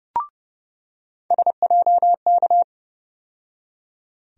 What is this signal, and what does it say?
Morse code, heard as a keyed tone, about 32 wpm, 700 Hz. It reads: SJK
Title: USA Call Sign Postfixes - 3 Letters 30wpm